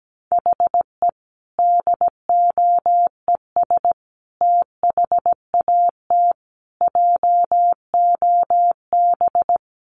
Morse code, heard as a keyed tone, about 17 wpm, 700 Hz. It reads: HE DOES THAT JOB
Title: Sentences from Top 700 Words 17wpm